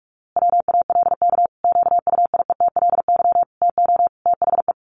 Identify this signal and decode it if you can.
Morse code, heard as a keyed tone, about 44 wpm, 700 Hz. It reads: WALK QUIETLY TO THE
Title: A Sound Of Thunder 45wpm